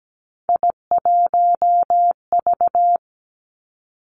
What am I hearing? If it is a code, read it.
Morse code, heard as a keyed tone, about 17 wpm, 700 Hz. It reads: I1V
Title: Three Challenging Letter-Number Combinations 17wpm